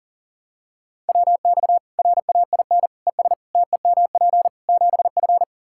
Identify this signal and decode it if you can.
Morse code, heard as a keyed tone, about 40 wpm, 700 Hz. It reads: WX RAIN ES TEMP 7F
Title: Practice QSO - Exchange 2 - Info - 40wpm